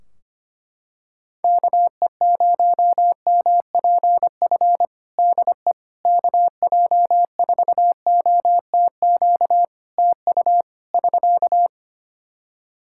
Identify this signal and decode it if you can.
Morse code, heard as a keyed tone, about 25 wpm, 700 Hz. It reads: KE0MPF DE KJ4OTQ TU <SK>
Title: Practice QSO - Exchange 4 - Ending - 25wpm